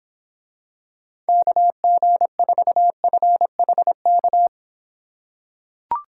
Morse code, heard as a keyed tone, about 26 wpm, 700 Hz. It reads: KG4FHK